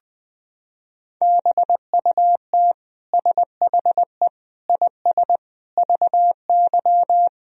Morse code, heard as a keyed tone, about 20 wpm, 700 Hz. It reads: BUT SHE IS VY
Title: Sentences from Top 400 Words and Abbreviations 20wpm